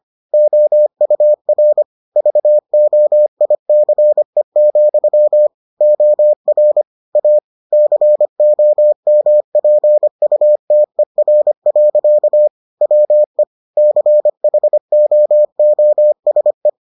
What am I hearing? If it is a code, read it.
Morse code, heard as a keyed tone, about 25 wpm, 600 Hz. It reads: OUR VOICE, OR A COMPUTER. WE CHOOSE